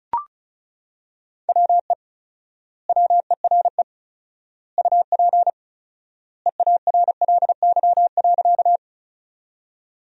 Morse code, heard as a keyed tone, about 34 wpm, 700 Hz. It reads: WE WERE UP EARLY.
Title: 4x Word-Spacing - Sentences from Top 400 Words - 35wpm